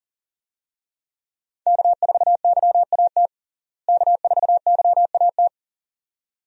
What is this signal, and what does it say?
Morse code, heard as a keyed tone, about 40 wpm, 700 Hz. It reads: K4YAT K4YAT
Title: USA Call Signs - 5 Characters - Repeated 2x - 40wpm